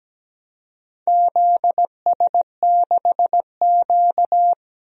Morse code, heard as a keyed tone, about 17 wpm, 700 Hz. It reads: ZS6Q
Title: DXCC-100 Call Signs - 4 Characters - 17wpm